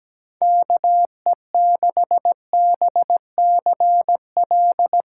Morse code, heard as a keyed tone, about 17 wpm, 700 Hz. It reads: KE6BCL
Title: USA Call Signs - 6 Characters - 6x Character Spacing - 17wpm